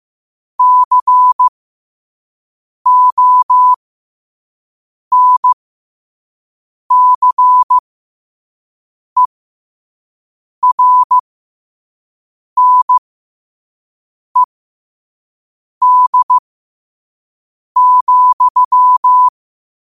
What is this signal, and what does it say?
Morse code, heard as a keyed tone, about 15 wpm, spaced as for 5 wpm, 1000 Hz. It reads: CONCERNED,
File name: qotdfiles / 5wpm